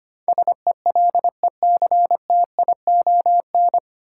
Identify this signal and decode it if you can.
Morse code, heard as a keyed tone, about 25 wpm, 700 Hz. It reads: SELECTION
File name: practice / 25wpm / 9 Letter Words - First 3 Letters Given - 25wpm